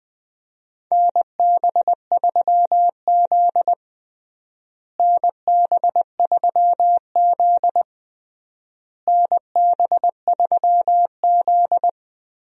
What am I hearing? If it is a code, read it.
Morse code, heard as a keyed tone, about 20 wpm, 700 Hz. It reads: NB3Z NB3Z NB3Z
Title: USA Call Signs - 4 Characters - Repeated 3x - 20wpm